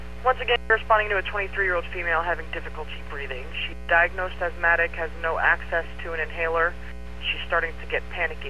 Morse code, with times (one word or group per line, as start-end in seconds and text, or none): none